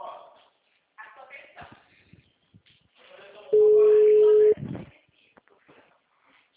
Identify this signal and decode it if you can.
Ringback tone (Europe, 425 Hz), with speech.